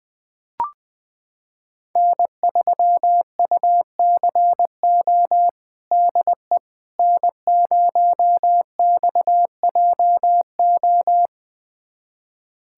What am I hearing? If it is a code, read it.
Morse code, heard as a keyed tone, about 21 wpm, 700 Hz. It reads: N3UCO DE N0XJO
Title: USA Callsign DE Callsign - 5 Characters - 20wpm